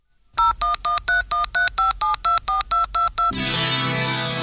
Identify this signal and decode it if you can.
DTMF, heard as telephone touch tones, tones of about 140 ms. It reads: 0113135724222